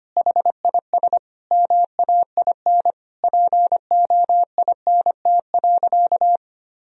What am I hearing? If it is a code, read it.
Morse code, heard as a keyed tone, about 25 wpm, 700 Hz. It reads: HIS MAIN POINT.
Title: Sentences from Top 1000 Words - Review Part 2 - 25wpm